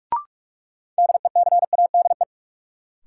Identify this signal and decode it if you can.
Morse code, heard as a keyed tone, about 46 wpm, 700 Hz. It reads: DECADE
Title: Sets of 5 Words - New Word Drill - 45wpm